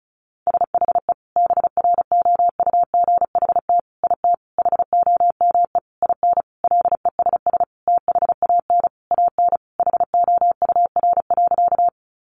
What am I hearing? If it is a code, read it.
Morse code, heard as a keyed tone, about 34 wpm, 700 Hz. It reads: SHE BROUGHT IT HOME IN LESS THAN AN HOUR.
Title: Sentences from Top 500 Words 35wpm